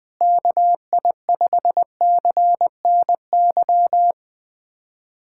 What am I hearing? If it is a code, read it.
Morse code, heard as a keyed tone, about 21 wpm, 700 Hz. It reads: KI5CNY
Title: USA Callsign DE Callsign - 6 Characters - 20wpm